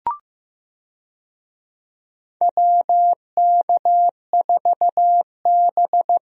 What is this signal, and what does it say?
Morse code, heard as a keyed tone, about 15 wpm, 700 Hz. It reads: WK4B
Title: USA Call Signs - 4 Characters - Repeated 3x - 15wpm